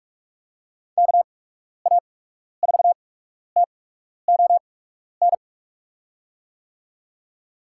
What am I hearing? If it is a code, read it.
Morse code, heard as a keyed tone, about 46 wpm, spaced as for 12 wpm, 700 Hz. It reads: KA4TON
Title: USA Call Signs - 6 Characters - 8x Character Spacing - 45wpm